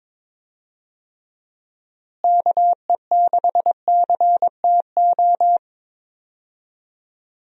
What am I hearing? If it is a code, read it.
Morse code, heard as a keyed tone, about 22 wpm, 700 Hz. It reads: KE6CTO
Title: USA Call Signs - 6 Characters - Repeated 2x - 22wpm